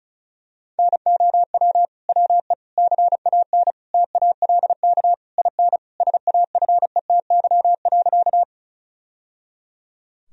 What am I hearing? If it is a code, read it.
Morse code, heard as a keyed tone, about 36 wpm, 700 Hz. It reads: NOW WE CAN TALK IN SAFETY.
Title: Sentences from Top 1000 Words - Review Part 4 - 35wpm